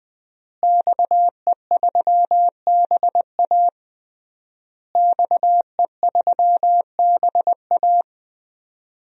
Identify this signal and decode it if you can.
Morse code, heard as a keyed tone, about 20 wpm, 700 Hz. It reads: XE3BA XE3BA
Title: DXCC-100 Call Signs - 5 Characters - Repeated 3x - 20wpm